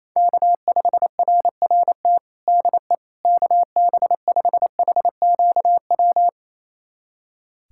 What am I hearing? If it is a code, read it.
Morse code, heard as a keyed tone, about 28 wpm, 700 Hz. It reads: K5RRT DE KB5HQW
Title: USA Callsign DE Callsign 28wpm